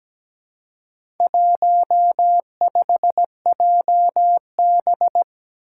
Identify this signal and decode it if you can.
Morse code, heard as a keyed tone, about 17 wpm, 700 Hz. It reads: 15JB
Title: Four Challenging Letter-Number Combinations 17wpm